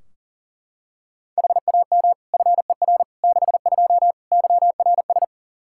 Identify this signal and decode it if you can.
Morse code, heard as a keyed tone, about 40 wpm, 700 Hz. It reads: HAM FER 62 YRS
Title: Practice QSO - Exchange 3 - More Info - 40wpm